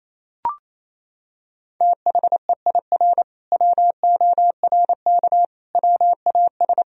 Morse code, heard as a keyed tone, about 28 wpm, 700 Hz. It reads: THEIR WORK WAS
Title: Sentences from Top 1000 Words - Review Part 3 - 28wpm